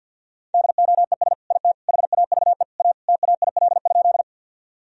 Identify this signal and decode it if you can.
Morse code, heard as a keyed tone, about 50 wpm, 700 Hz. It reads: DOES IT HAVE A TAIL?
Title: Sentences from Top 1000 Words - Review Part 3 - 50wpm